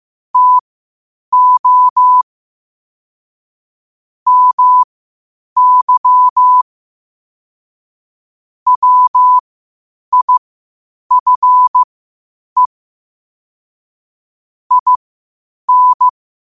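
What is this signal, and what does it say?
Morse code, heard as a keyed tone, about 15 wpm, spaced as for 8 wpm, 1000 Hz. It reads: TO MY WIFE IN
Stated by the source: Morse code quotes at 8 WPM.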